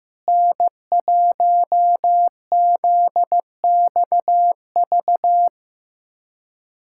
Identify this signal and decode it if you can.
Morse code, heard as a keyed tone, about 15 wpm, 700 Hz. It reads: N1ZXV
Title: USA Callsign DE Callsign - 5 Characters - 15wpm